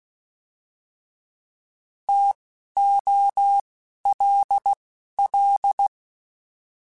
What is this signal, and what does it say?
Morse code, heard as a keyed tone, about 16 wpm, spaced as for 12 wpm, 800 Hz. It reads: TOLL